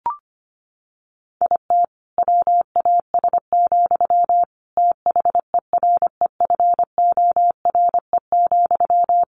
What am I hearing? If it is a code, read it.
Morse code, heard as a keyed tone, about 25 wpm, 700 Hz. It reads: IT WAS, THEREFORE,
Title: Sentences from Top 1500 Words 25wpm